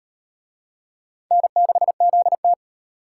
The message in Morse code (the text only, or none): N6ZT